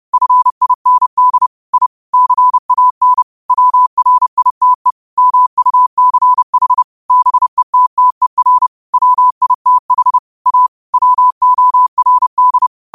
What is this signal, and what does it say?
Morse code, heard as a keyed tone, about 30 wpm, 1000 Hz. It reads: FIND I CAN WRITE MUCH BETTER WITH A WORD
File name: qotdfiles / 30wpm